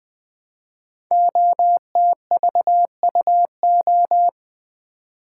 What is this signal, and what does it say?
Morse code, heard as a keyed tone, about 20 wpm, 700 Hz. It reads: OTVUO
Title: Code Groups - 5 Letter:Numbers - 20wpm